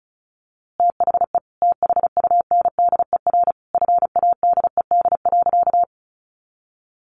Morse code, heard as a keyed tone, about 34 wpm, 700 Hz. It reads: THE THUNDER FADED.
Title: A Sound Of Thunder 35wpm